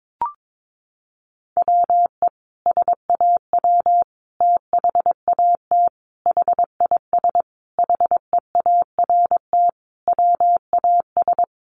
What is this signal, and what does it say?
Morse code, heard as a keyed tone, about 22 wpm, 700 Hz. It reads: WE SAW THAT HIS HEART WAS
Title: Sentences from Top 1200 Words 22wpm